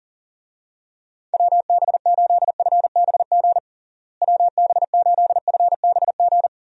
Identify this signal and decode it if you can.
Morse code, heard as a keyed tone, about 40 wpm, 700 Hz. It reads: WB8FBG WB8FBG
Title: USA Call Signs - 6 Characters - Repeated 2x - 40wpm